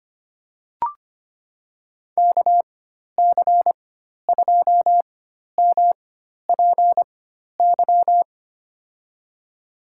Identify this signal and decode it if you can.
Morse code, heard as a keyed tone, about 25 wpm, spaced as for 12 wpm, 700 Hz. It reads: KC2MPY
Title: USA Call Signs - 6 Characters - 4x Character Spacing - 25wpm